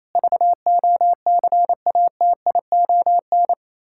Morse code, heard as a keyed tone, about 27 wpm, 700 Hz. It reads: VOCATION
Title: Pangram Challenge v2 - 28wpm